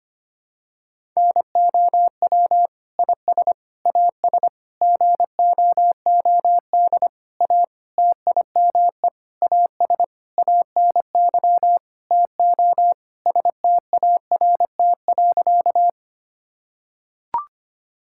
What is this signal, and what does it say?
Morse code, heard as a keyed tone, about 25 wpm, 700 Hz. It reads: NOW IS AS GOOD A TIME AS ANY TO START.
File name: practice / 25wpm / Sentences with 9-Word Phrases - 25wpm